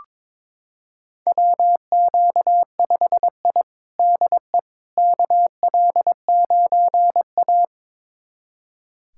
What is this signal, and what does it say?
Morse code, heard as a keyed tone, about 22 wpm, 700 Hz. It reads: WQ5I DE KL9A
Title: USA Callsign DE Callsign - 4 Characters - 22wpm